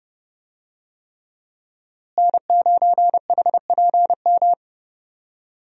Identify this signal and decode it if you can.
Morse code, heard as a keyed tone, about 30 wpm, 700 Hz. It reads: N9HPM
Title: USA Call Signs - 5 Characters - Repeated 3x - 30wpm